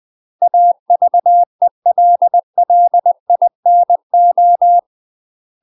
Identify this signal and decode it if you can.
Morse code, heard as a keyed tone, about 20 wpm, 700 Hz. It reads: AVELLINO
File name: UK MASTER QSOs TRAINING / 20WPM / UK MASTER QSOs 0017